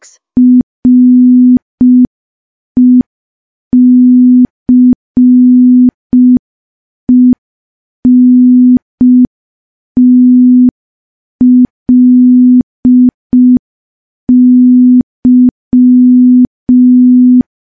Morse code, heard as a keyed tone, about 5 wpm, 250 Hz. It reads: RECENTLY
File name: morse / morse-training-05WPM